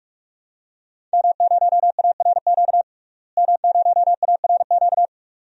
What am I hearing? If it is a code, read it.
Morse code, heard as a keyed tone, about 41 wpm, 700 Hz. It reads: M0ARQ M0ARQ
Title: DXCC-100 Call Signs - 5 Characters - Repeated 2x - 45wpm